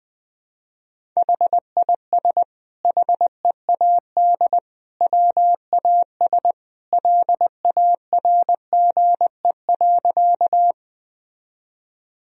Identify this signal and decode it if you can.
Morse code, heard as a keyed tone, about 21 wpm, 700 Hz. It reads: HIS HEAD WAS LARGE.
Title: Sentences from Top 1000 Words - Review Part 1 - 20wpm